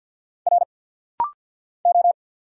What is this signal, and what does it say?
Morse code, heard as a keyed tone, about 50 wpm, 700 Hz. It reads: R O
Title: Single Letters - Rapid-Fire 50wpm